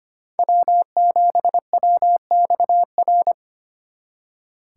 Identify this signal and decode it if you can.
Morse code, heard as a keyed tone, about 31 wpm, 700 Hz. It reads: W7WXR